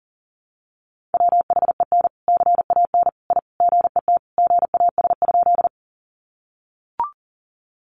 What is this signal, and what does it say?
Morse code, heard as a keyed tone, about 40 wpm, 700 Hz. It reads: WHEN CAN I GET GAS?
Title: Sentences from Top 1100 Words 40wpm